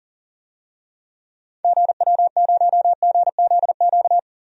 Morse code, heard as a keyed tone, about 40 wpm, 700 Hz. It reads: GW0GZQ
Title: DXCC-100 Call Signs - 6 Characters - 40wpm